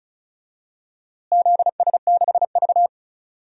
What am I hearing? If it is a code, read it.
Morse code, heard as a keyed tone, about 36 wpm, 700 Hz. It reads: ZS6V
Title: DXCC-100 Call Signs - 4 Characters - Repeated 2x - 35wpm